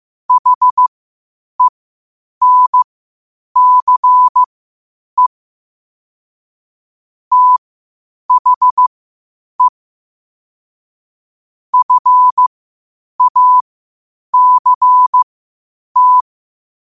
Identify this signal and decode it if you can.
Morse code, heard as a keyed tone, about 15 wpm, spaced as for 8 wpm, 1000 Hz. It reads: HENCE THE FACT
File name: qotdfiles / 8wpm